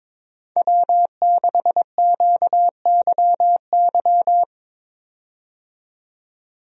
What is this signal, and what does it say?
Morse code, heard as a keyed tone, about 22 wpm, 700 Hz. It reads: W6QYY
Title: USA Call Signs - 5 Characters - 22wpm